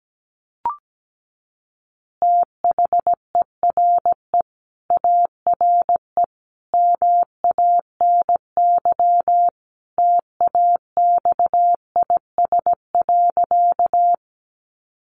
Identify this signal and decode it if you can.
Morse code, heard as a keyed tone, about 17 wpm, 700 Hz. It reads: THERE ARE MANY TAXIS.
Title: Easy Phrases 17wpm